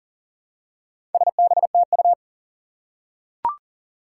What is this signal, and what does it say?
Morse code, heard as a keyed tone, about 41 wpm, 700 Hz. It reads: SBTU
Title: Four Challenging Letter-Number Combinations 40wpm